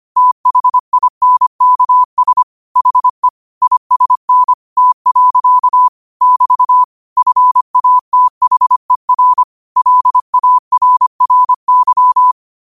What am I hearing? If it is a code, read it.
Morse code, heard as a keyed tone, about 25 wpm, 1000 Hz. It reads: THINKS HE ISN T. <BT> FATHER LARRY
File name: qotdfiles / 25wpm